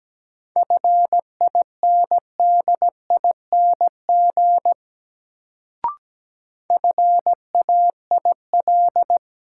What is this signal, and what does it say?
Morse code, heard as a keyed tone, about 17 wpm, 700 Hz. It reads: FINDING FAIL
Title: Sentences from Top 1100 Words - New Word Drill Alt - 17wpm